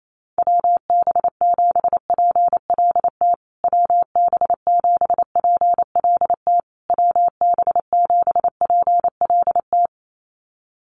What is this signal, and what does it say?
Morse code, heard as a keyed tone, about 28 wpm, 700 Hz. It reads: WB7PLT WB7PLT WB7PLT
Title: USA Call Signs Repeated 3x 28wpm